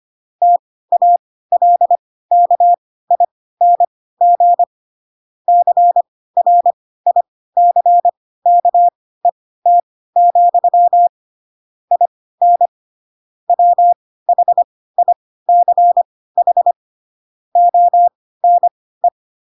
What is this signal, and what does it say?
Morse code, heard as a keyed tone, about 25 wpm, spaced as for 16 wpm, 700 Hz. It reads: TALKING CRICKET, IN WHICH ONE
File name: Pinocchio / Pinocchio ch4 10 25 ik1hge